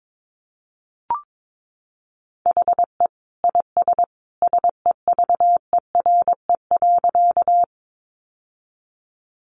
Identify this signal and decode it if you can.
Morse code, heard as a keyed tone, about 22 wpm, 700 Hz. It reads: HE IS SEVERE.
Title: Sentences from Top 1400 Words 22wpm